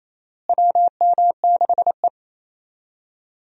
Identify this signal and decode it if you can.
Morse code, heard as a keyed tone, about 28 wpm, 700 Hz. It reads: WM6E
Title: USA Call Signs - 4 Characters - Repeated 2x - 28wpm